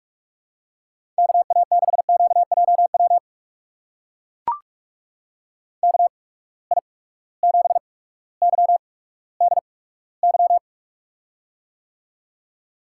Morse code, heard as a keyed tone, about 44 wpm, spaced as for 19 wpm, 700 Hz. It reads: KA6QJW KI7YDY
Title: USA Call Signs - 6 Characters - 8x Character Spacing - 45wpm